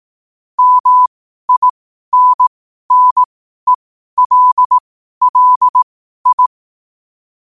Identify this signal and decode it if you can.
Morse code, heard as a keyed tone, about 18 wpm, spaced as for 12 wpm, 1000 Hz. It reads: MINNELLI
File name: qotdfiles / 12wpm